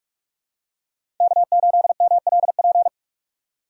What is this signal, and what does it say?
Morse code, heard as a keyed tone, about 46 wpm, 700 Hz. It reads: K8MLP
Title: USA Call Signs - 5 Characters - 6x Character Spacing - 45wpm